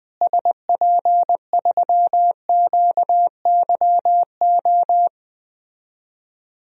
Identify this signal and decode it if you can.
Morse code, heard as a keyed tone, about 21 wpm, 700 Hz. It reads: SP3QYO
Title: DXCC-100 Call Signs - 6 Characters - Repeated 3x - 20wpm